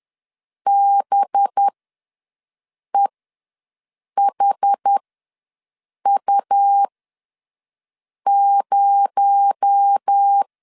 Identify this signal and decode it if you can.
Morse code, heard as a keyed tone, about 10 wpm, spaced as for 5 wpm, 800 Hz. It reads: BEHU0
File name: LEZIONI / 6 PKX0 / 6 PKX0 ws 10wpm 4220